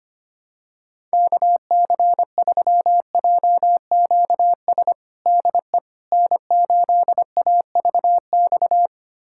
Morse code, heard as a keyed tone, about 25 wpm, 700 Hz. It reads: KC3JQS DE N8AVX
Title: USA Callsign DE Callsign 25wpm